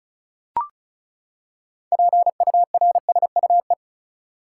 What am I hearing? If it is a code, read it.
Morse code, heard as a keyed tone, about 36 wpm, 700 Hz. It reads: PURSUE
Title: Six Letter Words - 35wpm